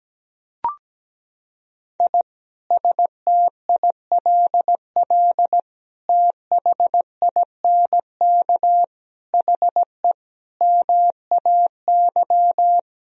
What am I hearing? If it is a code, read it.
Morse code, heard as a keyed tone, about 17 wpm, 700 Hz. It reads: I STILL THINK HE MAY
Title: Sentences from Top 1000 Words - Review Part 1 - 17wpm